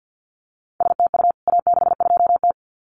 Morse code, heard as a keyed tone, about 50 wpm, 700 Hz. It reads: STU A62T
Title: CWT Exchange 50wpm